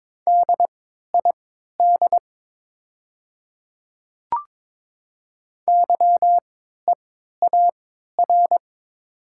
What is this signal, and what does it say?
Morse code, heard as a keyed tone, about 22 wpm, spaced as for 13 wpm, 700 Hz. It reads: DID YEAR
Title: 3x Character Spacing - Top 500 Words - 22wpm